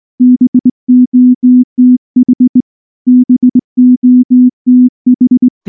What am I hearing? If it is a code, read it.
Morse code, heard as a keyed tone, about 19 wpm, 250 Hz. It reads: BOTH BOTH
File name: morse / morse-training-20WPM-1